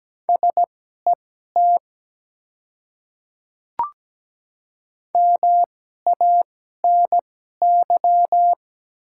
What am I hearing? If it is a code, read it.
Morse code, heard as a keyed tone, about 17 wpm, spaced as for 12 wpm, 700 Hz. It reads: SET MANY
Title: 2x Character Spacing - Top 200 Words - 17wpm